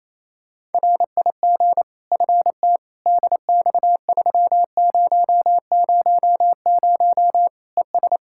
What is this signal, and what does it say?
Morse code, heard as a keyed tone, about 28 wpm, 700 Hz. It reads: RIG FT DX3000 ES
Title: K1USN SST QSO Practice 28wpm